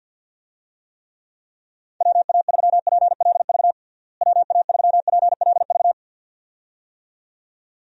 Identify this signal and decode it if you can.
Morse code, heard as a keyed tone, about 52 wpm, 700 Hz. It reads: WA3PLV WA3PLV
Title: USA Call Signs - 6 Characters - Repeated 2x - 50wpm